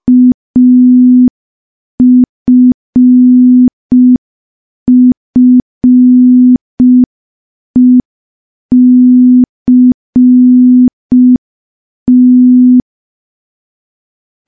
Morse code, heard as a keyed tone, about 5 wpm, 250 Hz. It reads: AFFECT
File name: morse / morse-training-05WPM